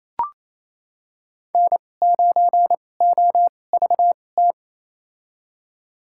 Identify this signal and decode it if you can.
Morse code, heard as a keyed tone, about 28 wpm, spaced as for 20 wpm, 700 Hz. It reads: N9OVT
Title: USA Call Signs - 5 Characters - 2x Character Spacing - 28wpm